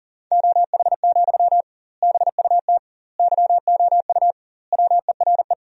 Morse code, heard as a keyed tone, about 37 wpm, 700 Hz. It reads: OH, BUT YOU WERE
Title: 2x Word-Spacing - Sentences from Top 500 Words - 40wpm